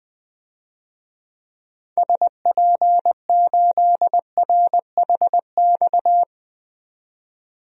Morse code, heard as a keyed tone, about 20 wpm, 700 Hz. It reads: SP8RHX
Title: DXCC-100 Call Signs - 6 Characters - 20wpm